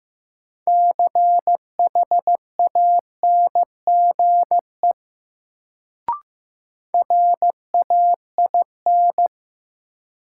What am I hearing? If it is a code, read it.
Morse code, heard as a keyed tone, about 15 wpm, 700 Hz. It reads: CHANGE RAIN
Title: Sentences from Top 600 Words - New Word Drill - 15wpm